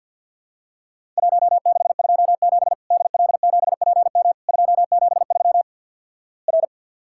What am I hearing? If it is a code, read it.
Morse code, heard as a keyed tone, about 50 wpm, 700 Hz. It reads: 1627 DL7PM 273 R
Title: Worked All Europe DX Contest - QTC Exchange - 50wpm